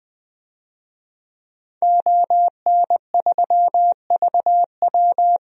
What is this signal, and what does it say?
Morse code, heard as a keyed tone, about 20 wpm, 700 Hz. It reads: ON3VW
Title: DXCC-100 Call Signs - 5 Characters - 20wpm